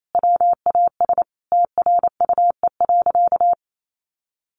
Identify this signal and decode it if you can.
Morse code, heard as a keyed tone, about 28 wpm, 700 Hz. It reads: WAS TRUE.
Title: Sentences from Top 400 Words 28wpm